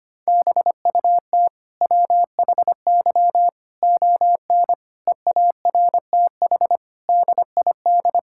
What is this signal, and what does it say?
Morse code, heard as a keyed tone, about 25 wpm, 700 Hz. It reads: BUT WHY ON EARTH DID
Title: Sentences from Top 1000 Words - Review Part 6 - 25wpm